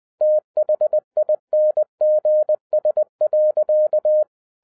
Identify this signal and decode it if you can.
Morse code, heard as a keyed tone, about 20 wpm, 610 Hz. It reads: THINGS.
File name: 2011-07-mp3 / n58 BACK FROM A BIG ACCIDENT 20wpm